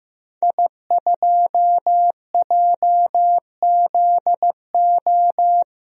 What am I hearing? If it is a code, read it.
Morse code, heard as a keyed tone, about 16 wpm, 700 Hz. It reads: I2JZO
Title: DXCC-100 Call Signs - 5 Characters - 15wpm